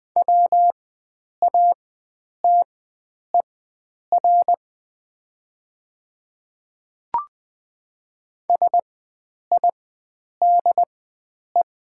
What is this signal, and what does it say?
Morse code, heard as a keyed tone, about 20 wpm, spaced as for 9 wpm, 700 Hz. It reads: WATER SIDE
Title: 4x Character Spacing - Top 100 Words - 20wpm